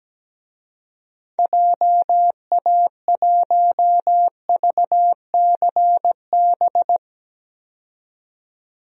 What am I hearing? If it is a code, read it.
Morse code, heard as a keyed tone, about 17 wpm, 700 Hz. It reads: JA1VCB
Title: DXCC-100 Call Signs - 6 Characters - 17wpm